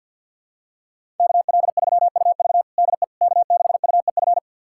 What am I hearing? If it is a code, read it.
Morse code, heard as a keyed tone, about 52 wpm, 700 Hz. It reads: KL3UV DE K6UEF
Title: USA Callsign DE Callsign - 5 Characters - 50wpm